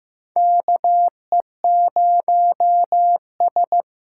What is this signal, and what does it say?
Morse code, heard as a keyed tone, about 15 wpm, 700 Hz. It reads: KE0S
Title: USA Call Signs - 6 Characters - 8x Character Spacing - 15wpm